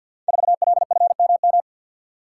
Morse code, heard as a keyed tone, about 52 wpm, 700 Hz. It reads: 4LFMM
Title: Code Groups - 5 Letter:Numbers - 50wpm